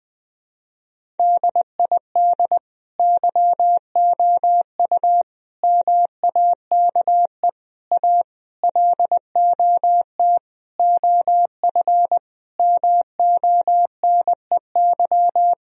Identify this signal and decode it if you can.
Morse code, heard as a keyed tone, about 20 wpm, 700 Hz. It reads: DID YOU MAKE A LOT OF MONEY
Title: Sentences from Top 1000 Words - Review Part 4 - 20wpm